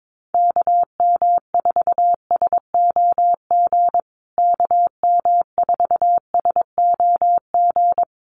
Morse code, heard as a keyed tone, about 22 wpm, 700 Hz. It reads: KM4SOG KM4SOG
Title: USA Call Signs Repeated 3x 22wpm